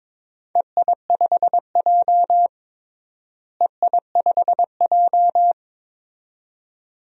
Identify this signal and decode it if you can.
Morse code, heard as a keyed tone, about 22 wpm, 700 Hz. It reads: EI5J EI5J
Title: DXCC-100 Call Signs - 4 Characters - Repeated 2x - 22wpm